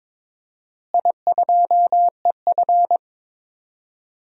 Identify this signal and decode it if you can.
Morse code, heard as a keyed tone, about 22 wpm, 700 Hz. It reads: I2EF
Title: DXCC-100 Call Signs - 4 Characters - Repeated 3x - 22wpm